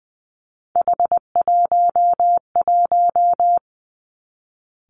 Morse code, heard as a keyed tone, about 20 wpm, 700 Hz. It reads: H11
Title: Three Challenging Letter-Number Combinations 20wpm